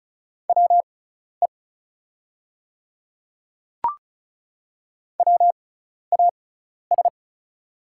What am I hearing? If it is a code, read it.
Morse code, heard as a keyed tone, about 36 wpm, spaced as for 12 wpm, 690 Hz. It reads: WE WAS
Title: 6x Character Spacing - Top 100 Words - 35wpm